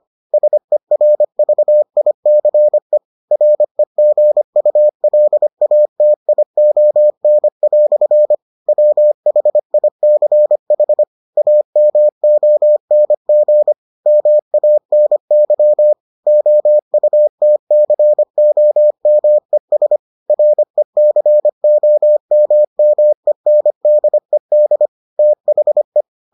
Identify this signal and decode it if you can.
Morse code, heard as a keyed tone, about 25 wpm, 600 Hz. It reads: SERVICE REGULATION" WHICH AMONG MANY OUTCOMES RECOMMENDED THE